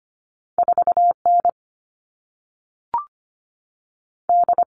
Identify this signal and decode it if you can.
Morse code, heard as a keyed tone, about 25 wpm, 700 Hz. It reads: 4N D
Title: All Cut Numbers - 2-Digit 25wpm